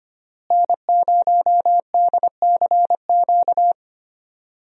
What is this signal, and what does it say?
Morse code, heard as a keyed tone, about 25 wpm, 700 Hz. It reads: N0DCQ
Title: USA Call Signs - 5 Characters - Repeated 3x - 25wpm